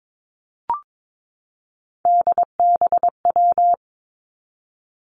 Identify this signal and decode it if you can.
Morse code, heard as a keyed tone, about 22 wpm, 700 Hz. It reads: DBW
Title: Three Challenging Letter-Number Combinations 22wpm